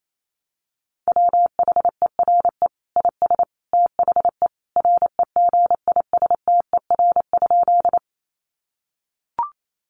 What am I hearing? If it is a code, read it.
Morse code, heard as a keyed tone, about 28 wpm, 700 Hz. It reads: WHERE IS THE REGISTER?
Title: Sentences from Top 1200 Words 28wpm